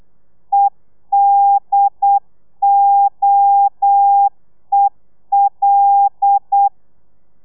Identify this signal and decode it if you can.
Morse code, heard as a keyed tone, about 8 wpm, 800 Hz. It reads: EDOEL
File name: telegrafia / i2tfj / Lez5 35